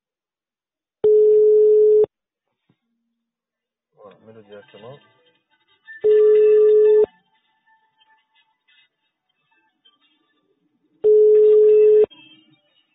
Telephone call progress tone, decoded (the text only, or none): ringback tone